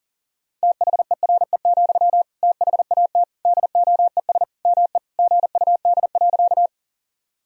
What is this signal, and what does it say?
Morse code, heard as a keyed tone, about 40 wpm, 700 Hz. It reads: THERE, THAT DOES ME GUD.
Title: Sentences from Top 200 Words and Abbreviations 40wpm